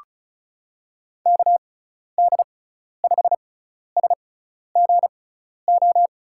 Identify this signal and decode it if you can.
Morse code, heard as a keyed tone, about 35 wpm, spaced as for 12 wpm, 700 Hz. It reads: KD5SGO
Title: USA Call Signs - 6 Characters - 6x Character Spacing - 35wpm